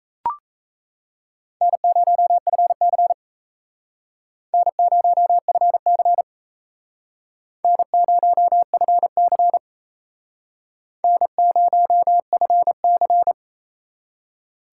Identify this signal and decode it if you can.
Morse code, heard as a keyed tone, about 35 wpm, 700 Hz. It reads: N0FC N0FC N0FC N0FC
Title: Speed Racing - USA 1x2 Call Signs 28wpm